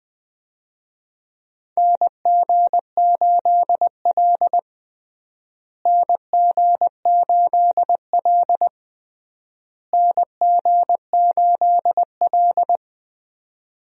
Morse code, heard as a keyed tone, about 20 wpm, 700 Hz. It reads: NG8L NG8L NG8L
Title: USA Call Signs - 4 Characters - Repeated 3x - 20wpm